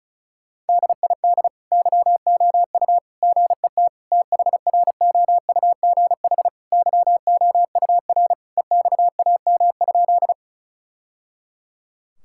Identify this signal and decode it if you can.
Morse code, heard as a keyed tone, about 34 wpm, 700 Hz. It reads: DID YOU GET THROUGH YOUR EXAM?
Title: Sentences from Top 1000 Words - Review Part 5 - 35wpm